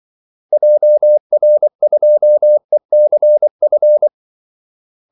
Morse code, heard as a keyed tone, about 24 wpm, 600 Hz. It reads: JR2ECF
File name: Callsigns-24wpm